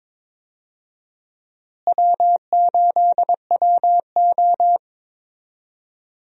Morse code, heard as a keyed tone, about 22 wpm, 700 Hz. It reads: W8WO
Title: USA Call Signs - 4 Characters - Repeated 2x - 22wpm